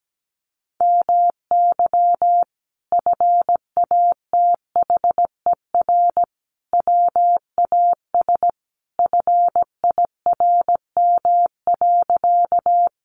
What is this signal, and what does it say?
Morse code, heard as a keyed tone, about 17 wpm, 700 Hz. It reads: MY FATHER WAS FIRM.
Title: Sentences from Top 1400 Words 17wpm